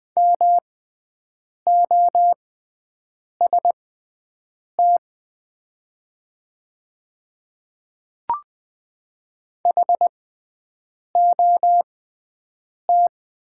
Morse code, heard as a keyed tone, about 20 wpm, spaced as for 7 wpm, 700 Hz. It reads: MOST HOT
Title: 6x Character Spacing - Top 200 Words - 20wpm